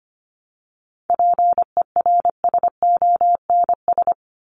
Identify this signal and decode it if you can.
Morse code, heard as a keyed tone, about 25 wpm, 700 Hz. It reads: PERSONS
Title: ES:S Suffix - Top 100 Words with Base Words 25wpm